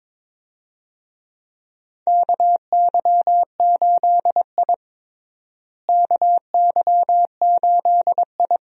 Morse code, heard as a keyed tone, about 22 wpm, 700 Hz. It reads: KY8I KY8I
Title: USA Call Signs - 4 Characters - Repeated 3x - 22wpm